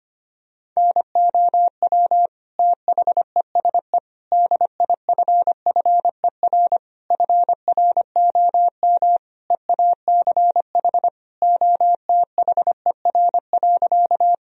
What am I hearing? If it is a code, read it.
Morse code, heard as a keyed tone, about 25 wpm, 700 Hz. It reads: NOW THESE DIFFER FROM EACH OTHER.
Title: Sentences from Top 1000 Words - Review Part 1 - 25wpm